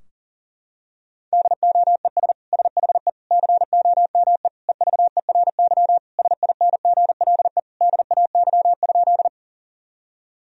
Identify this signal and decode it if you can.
Morse code, heard as a keyed tone, about 40 wpm, 700 Hz. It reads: DOES SHE COME EVERY SINGLE DAY?
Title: Sentences from Top 1000 Words - Review Part 4 - 40wpm